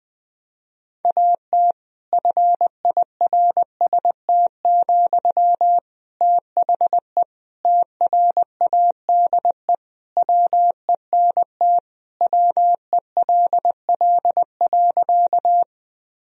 Morse code, heard as a keyed tone, about 20 wpm, 700 Hz. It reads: AT FIRST, THE TRADE WENT WELL.
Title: Sentences from Top 1000 Words - Review Part 4 - 20wpm